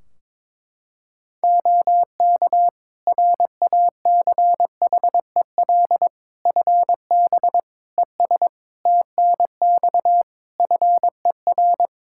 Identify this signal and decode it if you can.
Morse code, heard as a keyed tone, about 21 wpm, 700 Hz. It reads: OK RACHEL FB ES TNX FER
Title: Practice QSO - Exchange 2 - Info - 22wpm